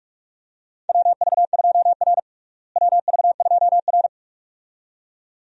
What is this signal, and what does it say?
Morse code, heard as a keyed tone, about 47 wpm, 700 Hz. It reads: WV2R WV2R
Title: USA Call Signs - 4 Characters - Repeated 2x - 45wpm